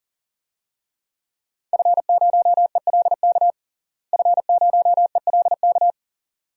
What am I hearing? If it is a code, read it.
Morse code, heard as a keyed tone, about 40 wpm, 700 Hz. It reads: F0ELK F0ELK
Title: DXCC-100 Call Signs - 5 Characters - Repeated 2x - 40wpm